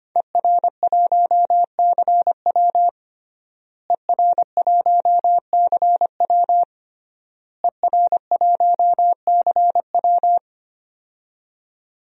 Morse code, heard as a keyed tone, about 25 wpm, 700 Hz. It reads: ER1CW ER1CW ER1CW
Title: DXCC-100 Call Signs - 5 Characters - Repeated 3x - 25wpm